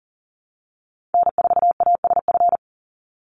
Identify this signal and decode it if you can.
Morse code, heard as a keyed tone, about 40 wpm, 700 Hz. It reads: N4ASF
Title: Call Signs Repeated 3x - 40wpm